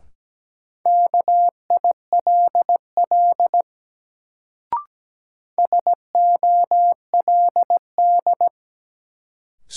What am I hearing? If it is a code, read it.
Morse code, heard as a keyed tone, about 17 wpm, 700 Hz. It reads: KILL SOLD
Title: Sentences from Top 1400 Words - New Word Drill - 17wpm